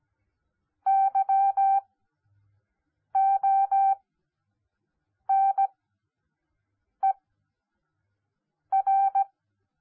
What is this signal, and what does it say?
Morse code, heard as a keyed tone, about 17 wpm, spaced as for 5 wpm, 780 Hz. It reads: YONER
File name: Onlinelehrgang / morsen / DJ4UF Morsen10